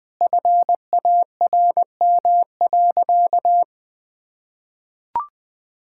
Morse code, heard as a keyed tone, about 20 wpm, 700 Hz. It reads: FARM.
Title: Sentences from Top 1000 Words - Review Part 5 - 20wpm